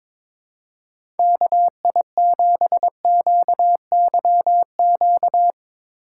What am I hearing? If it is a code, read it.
Morse code, heard as a keyed tone, about 21 wpm, 700 Hz. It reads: KI7QYQ
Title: USA Call Signs - 6 Characters - 2x Character Spacing - 22wpm